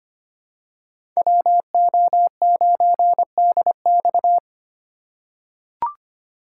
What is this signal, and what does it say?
Morse code, heard as a keyed tone, about 25 wpm, 700 Hz. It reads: WO9DX
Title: USA Call Signs - 3x Character Spacing - 25wpm